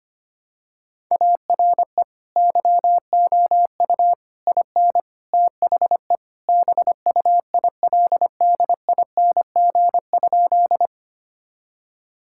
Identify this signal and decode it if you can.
Morse code, heard as a keyed tone, about 25 wpm, 700 Hz. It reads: ARE YOU IN THE BUILDING?
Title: Sentences from Top 1000 Words - Review Part 5 - 25wpm